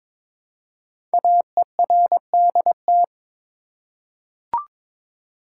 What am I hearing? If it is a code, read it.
Morse code, heard as a keyed tone, about 22 wpm, 700 Hz. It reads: AERDT
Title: Code Groups - 5 Letters - 22wpm